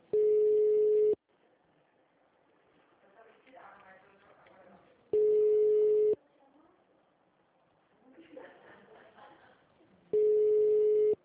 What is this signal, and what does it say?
Ringback tone (Europe, 425 Hz), with speech.
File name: out- -2014- - - .173